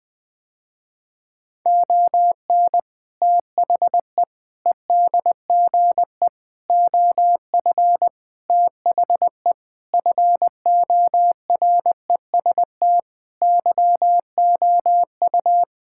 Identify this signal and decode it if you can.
Morse code, heard as a keyed tone, about 20 wpm, 700 Hz. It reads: ON THE EDGE OF THE FOREST YOU
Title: Sentences with 4-Word Phrases - Spoken Beginning - 20wpm